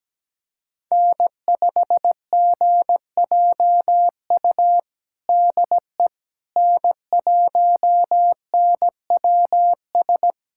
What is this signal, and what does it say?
Morse code, heard as a keyed tone, about 17 wpm, 700 Hz. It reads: N5GJU DE N1NWS
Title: USA Callsign DE Callsign - 5 Characters - 17wpm